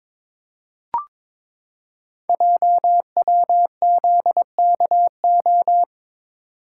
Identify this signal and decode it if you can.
Morse code, heard as a keyed tone, about 22 wpm, 700 Hz. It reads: JWZKO
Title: Code Groups - 5 Letters - 22wpm